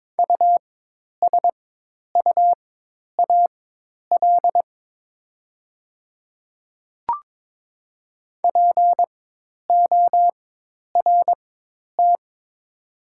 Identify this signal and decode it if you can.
Morse code, heard as a keyed tone, about 22 wpm, spaced as for 10 wpm, 700 Hz. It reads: USUAL PORT
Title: 4x Character Spacing - Top 500 Words - 22wpm